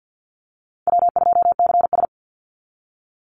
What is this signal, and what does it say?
Morse code, heard as a keyed tone, about 48 wpm, 700 Hz. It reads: W2CS
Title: Call Signs - 50wpm